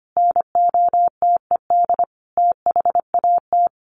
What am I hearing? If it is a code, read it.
Morse code, heard as a keyed tone, about 25 wpm, 700 Hz. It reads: NOTED THAT
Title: Sets of 5 Words 25wpm